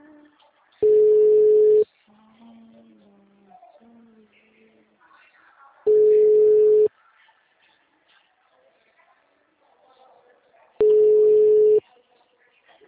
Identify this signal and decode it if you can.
Ringback tone (Europe, 425 Hz), with speech.
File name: out- -2006- - - .76427